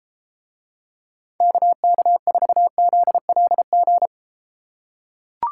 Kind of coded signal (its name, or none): Morse code